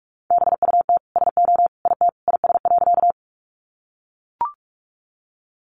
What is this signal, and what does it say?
Morse code, heard as a keyed tone, about 44 wpm, 700 Hz. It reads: BUT SO IT IS.
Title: Sentences from Top 100 Words 45wpm